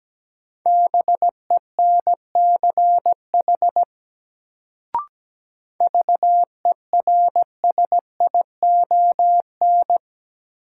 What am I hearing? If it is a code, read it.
Morse code, heard as a keyed tone, about 17 wpm, 700 Hz. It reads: BENCH VERSION
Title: Sentences from Top 1400 Words - New Word Drill - 17wpm